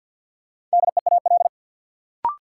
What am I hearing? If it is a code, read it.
Morse code, heard as a keyed tone, about 53 wpm, 700 Hz. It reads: DEAL
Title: Sets of 3 Words - New Word Drill - 50wpm